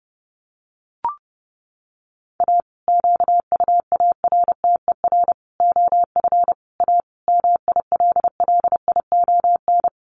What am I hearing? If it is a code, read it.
Morse code, heard as a keyed tone, about 30 wpm, 700 Hz. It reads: A QUARTER OF A MILLION
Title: Sets of 5 Words Encore2 30wpm